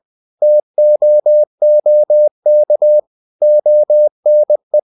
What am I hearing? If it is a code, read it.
Morse code, heard as a keyed tone, about 20 wpm, 600 Hz. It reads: TOOK ONE
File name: the old man and the sea / the old man and the sea part 1 20wpm